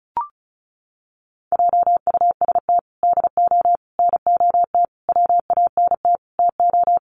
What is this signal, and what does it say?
Morse code, heard as a keyed tone, about 34 wpm, 700 Hz. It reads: JUST DO NOT WANT TO
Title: Sets of 5 Words Encore 35wpm